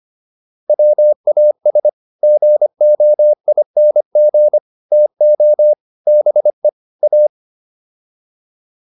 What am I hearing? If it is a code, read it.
Morse code, heard as a keyed tone, about 25 wpm, 600 Hz. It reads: WAS GOING TO BE A